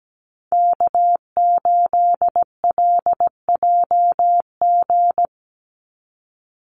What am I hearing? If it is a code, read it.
Morse code, heard as a keyed tone, about 17 wpm, 700 Hz. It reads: K8LJG
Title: NA SOTA Chaser Callsigns 17wpm